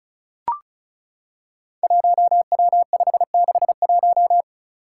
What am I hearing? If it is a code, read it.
Morse code, heard as a keyed tone, about 36 wpm, 700 Hz. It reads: 1W561
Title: Five Challenging Letter-Number Combinations 35wpm